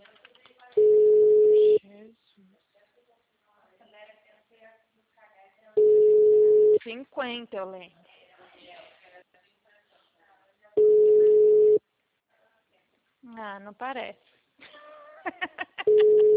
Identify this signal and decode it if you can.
Ringback tone (Europe, 425 Hz), with speech.